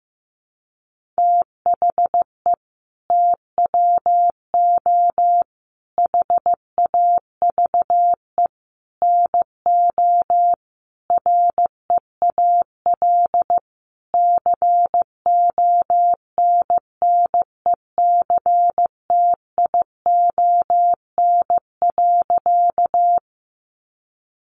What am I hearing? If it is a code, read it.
Morse code, heard as a keyed tone, about 14 wpm, 700 Hz. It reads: THE TWO HAVE NO REAL CONNECTION.
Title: Sentences from Top 1300 Words 15wpm